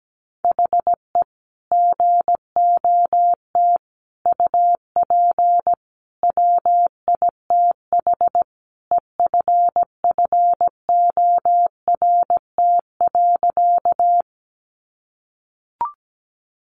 Morse code, heard as a keyed tone, about 17 wpm, 700 Hz. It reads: HE GOT UP WITH EFFORT.
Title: Sentences from Top 1200 Words 17wpm